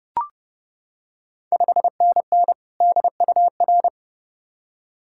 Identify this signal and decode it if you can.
Morse code, heard as a keyed tone, about 30 wpm, 700 Hz. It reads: 5NN DUR
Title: Ontario QSO Party Exchange 30wpm